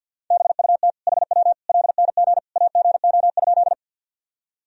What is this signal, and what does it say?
Morse code, heard as a keyed tone, about 52 wpm, 700 Hz. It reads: BUT HW LNG AGO?
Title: Sentences from Top 500 Words and Abbreviations 50wpm